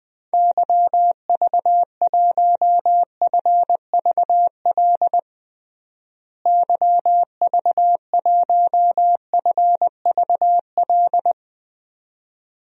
Morse code, heard as a keyed tone, about 20 wpm, 700 Hz. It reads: YV1FVL YV1FVL
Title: DXCC-100 Call Signs - 6 Characters - Repeated 2x - 20wpm